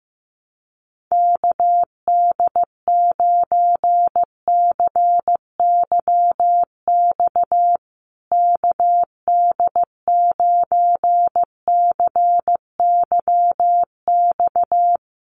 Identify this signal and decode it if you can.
Morse code, heard as a keyed tone, about 15 wpm, 700 Hz. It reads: KD9CYX KD9CYX
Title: USA Call Signs Repeated 3x 15wpm